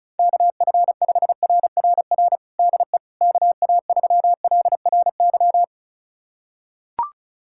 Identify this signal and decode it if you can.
Morse code, heard as a keyed tone, about 36 wpm, 700 Hz. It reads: KF5RRR DE KA3LRY
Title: USA Callsign DE Callsign - 6 Characters - 35wpm